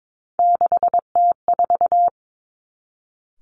Morse code, heard as a keyed tone, about 22 wpm, 700 Hz. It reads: 6T4
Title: Three Challenging Letter-Number Combinations 22wpm